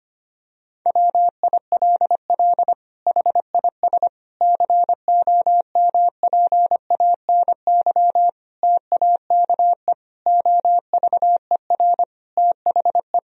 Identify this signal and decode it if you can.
Morse code, heard as a keyed tone, about 25 wpm, 700 Hz. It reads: WILL HIS COMPANY TAKE OVER THE
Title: Sentences from Top 1000 Words - Review Part 4 - 25wpm